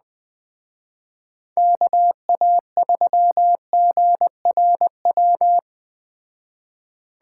Morse code, heard as a keyed tone, about 20 wpm, 700 Hz. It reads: KA3GRW